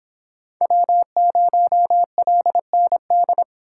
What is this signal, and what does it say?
Morse code, heard as a keyed tone, about 26 wpm, 700 Hz. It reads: W0LND